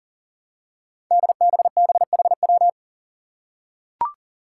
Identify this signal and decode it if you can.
Morse code, heard as a keyed tone, about 40 wpm, 700 Hz. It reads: DBBHW
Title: Five Challenging Letter-Number Combinations 40wpm